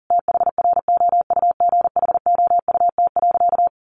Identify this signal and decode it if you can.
Morse code, heard as a keyed tone, about 40 wpm, 700 Hz. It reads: THROUGHOUT.
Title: Sentences from Top 800 Words 40wpm